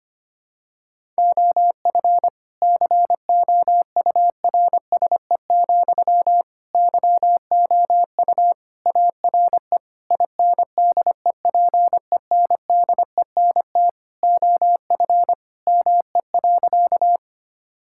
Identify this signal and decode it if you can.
Morse code, heard as a keyed tone, about 25 wpm, 700 Hz. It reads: OF COURSE, YOU ARE INDEPENDENT OF ME.
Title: Sentences from Top 1000 Words - Review Part 6 - 25wpm